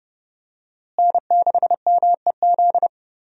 Morse code, heard as a keyed tone, about 30 wpm, 700 Hz. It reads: N6MEZ
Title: USA Call Signs - 5 Characters - Repeated 2x - 30wpm